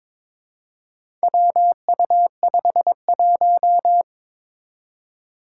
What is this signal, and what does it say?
Morse code, heard as a keyed tone, about 22 wpm, 700 Hz. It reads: WU51
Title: Four Challenging Letter-Number Combinations 22wpm